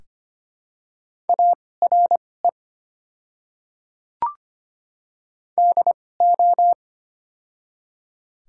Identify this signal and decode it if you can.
Morse code, heard as a keyed tone, about 25 wpm, spaced as for 18 wpm, 700 Hz. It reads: ARE DO
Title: 2x Character Spacing - Top 100 Words - 25wpm